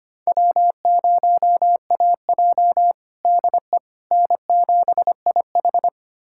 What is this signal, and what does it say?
Morse code, heard as a keyed tone, about 25 wpm, 700 Hz. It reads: W0AJ DE N7IH
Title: USA Callsign DE Callsign - 4 Characters - 25wpm